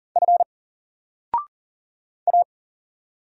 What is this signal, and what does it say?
Morse code, heard as a keyed tone, about 40 wpm, 700 Hz. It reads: F A
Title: Single Letter-Number 40wpm